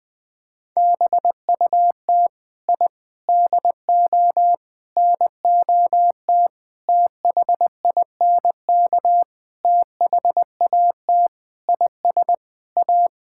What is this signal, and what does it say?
Morse code, heard as a keyed tone, about 20 wpm, 700 Hz. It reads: BUT I DO NOT THINK THAT IS A
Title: Sentences with 5-Word Phrases - 20wpm